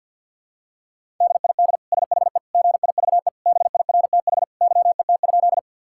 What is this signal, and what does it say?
Morse code, heard as a keyed tone, about 50 wpm, 700 Hz. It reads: DID SHE GIVE BIRTH YET?
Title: Sentences from Top 1000 Words - Review Part 5 - 50wpm